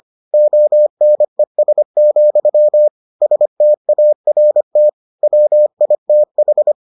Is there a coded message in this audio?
Yes: Morse code from 0.3 to 6.7 s, heard as a keyed tone, about 25 wpm, 600 Hz. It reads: ONES, START WITH